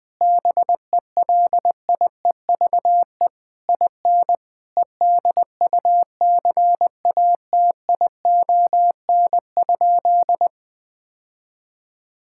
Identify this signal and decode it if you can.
Morse code, heard as a keyed tone, about 20 wpm, 700 Hz. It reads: BELIEVE IN EDUCATION?
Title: Sentences from Top 1000 Words - Review Part 6 - 20wpm